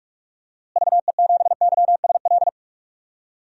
Morse code, heard as a keyed tone, about 46 wpm, 700 Hz. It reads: VE7YSL
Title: DXCC-100 Call Signs - 6 Characters - Repeated 2x - 45wpm